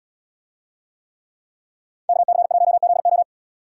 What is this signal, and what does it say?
Morse code, heard as a keyed tone, about 75 wpm, 690 Hz. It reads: KC1ZO